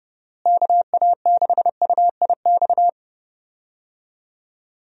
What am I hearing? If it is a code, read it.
Morse code, heard as a keyed tone, about 30 wpm, 700 Hz. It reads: KA6UIX
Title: USA Call Signs - 6 Characters - Repeated 2x - 30wpm